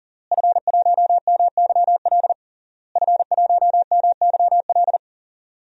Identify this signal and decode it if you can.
Morse code, heard as a keyed tone, about 40 wpm, 700 Hz. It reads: F1MYL F1MYL
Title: DXCC-100 Call Signs - 5 Characters - Repeated 3x - 40wpm